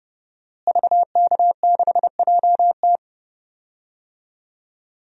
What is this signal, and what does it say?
Morse code, heard as a keyed tone, about 30 wpm, 700 Hz. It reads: VK6JT
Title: DXCC-100 Call Signs - 5 Characters - Repeated 2x - 30wpm